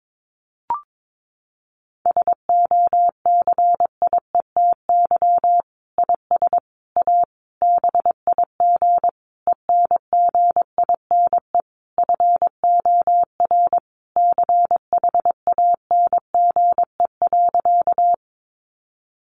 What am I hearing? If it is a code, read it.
Morse code, heard as a keyed tone, about 22 wpm, 700 Hz. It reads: SOCIETY IS A BIG ENGINE FOR CHANGE.
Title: Sentences from Top 1500 Words 22wpm